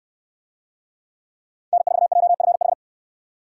Morse code, heard as a keyed tone, about 67 wpm, 700 Hz. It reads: N4JFH